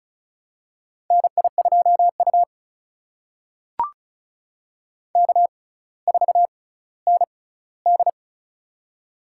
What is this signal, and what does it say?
Morse code, heard as a keyed tone, about 34 wpm, spaced as for 18 wpm, 700 Hz. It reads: NI2U K4ND
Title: USA Call Signs - 4 Characters - 6x Character Spacing - 35wpm